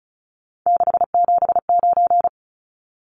Morse code, heard as a keyed tone, about 34 wpm, 700 Hz. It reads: 679
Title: Numbers - 3-Digit 35wpm